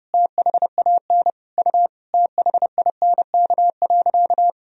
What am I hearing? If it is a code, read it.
Morse code, heard as a keyed tone, about 32 wpm, 700 Hz. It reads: THAN U THINK.
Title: Sentences from Top 400 Words and Abbreviations 30wpm